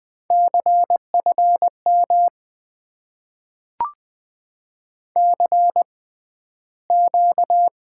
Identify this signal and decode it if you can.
Morse code, heard as a keyed tone, about 20 wpm, spaced as for 12 wpm, 700 Hz. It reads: CFM CQ
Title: Top 90 QSO Elements - 6x Character Spacing - 20wpm